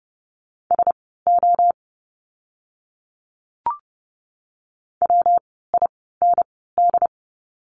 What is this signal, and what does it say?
Morse code, heard as a keyed tone, about 30 wpm, spaced as for 17 wpm, 700 Hz. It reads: SO WIND
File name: course / 30:10wpm / 115 Lesson 19f - Letter w - Words 30wpm